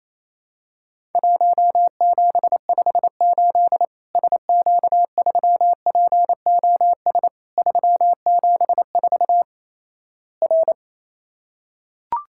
Morse code, heard as a keyed tone, about 28 wpm, 700 Hz. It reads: 1758 SQ3POS 374 R
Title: Worked All Europe DX Contest - QTC Exchange - 28wpm